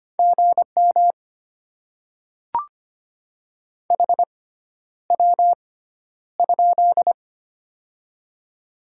Morse code, heard as a keyed tone, about 25 wpm, spaced as for 11 wpm, 700 Hz. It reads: GM HW?
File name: Top 20 QSO Elements - 6x Character Spacing - 25wpm